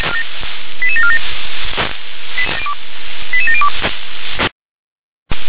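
Five-tone selective call (ZVEI), heard as quick runs of five tones, tones of about 70 ms. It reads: 88747 88702
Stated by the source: Received by SDR, ZVEI five-tone selective call.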